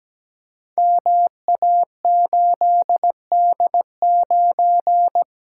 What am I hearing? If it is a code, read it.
Morse code, heard as a keyed tone, about 17 wpm, 700 Hz. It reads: MA8D9
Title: Code Groups - 5 Letter:Numbers - 17wpm